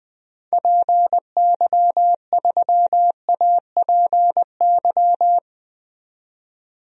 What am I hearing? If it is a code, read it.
Morse code, heard as a keyed tone, about 20 wpm, 700 Hz. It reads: PY3APY
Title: DXCC-100 Call Signs - 6 Characters - 20wpm